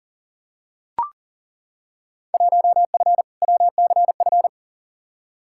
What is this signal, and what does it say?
Morse code, heard as a keyed tone, about 40 wpm, 700 Hz. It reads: 1F WCF
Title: Field Day Exchanges - 40wpm